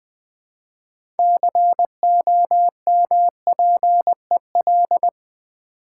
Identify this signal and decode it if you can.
Morse code, heard as a keyed tone, about 20 wpm, 700 Hz. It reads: COMPEL
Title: Build A Word - 2 Letters At A Time - 6-Letter Words - 20wpm